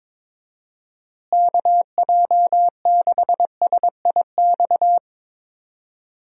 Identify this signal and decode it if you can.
Morse code, heard as a keyed tone, about 22 wpm, 700 Hz. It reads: KJ6SIX
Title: USA Call Signs - 6 Characters - Repeated 2x - 22wpm